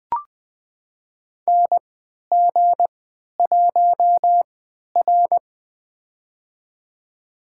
Morse code, heard as a keyed tone, about 20 wpm, spaced as for 11 wpm, 700 Hz. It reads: NG1R
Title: USA Call Signs - 4 Characters - 3x Character Spacing - 20wpm